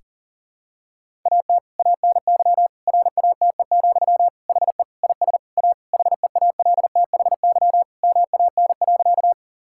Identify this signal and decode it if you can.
Morse code, heard as a keyed tone, about 40 wpm, 700 Hz. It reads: AT ANY RATE, HE IS A HEALTHY MAN.
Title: Sentences from Top 1000 Words - Review Part 5 - 40wpm